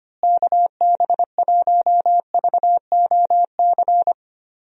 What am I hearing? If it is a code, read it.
Morse code, heard as a keyed tone, about 25 wpm, 700 Hz. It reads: KB1VOC
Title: USA Call Signs - 6 Characters - Repeated 2x - 25wpm